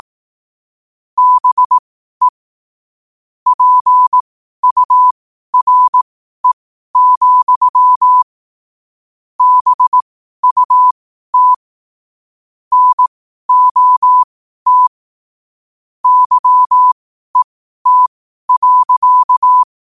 Morse code, heard as a keyed tone, about 18 wpm, spaced as for 12 wpm, 1000 Hz. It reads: BE PURE, BUT NOT YET.
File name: qotdfiles / 12wpm